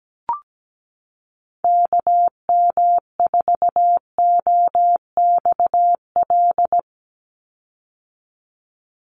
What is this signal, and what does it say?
Morse code, heard as a keyed tone, about 17 wpm, 700 Hz. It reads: KM4OXL
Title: USA Call Signs 17wpm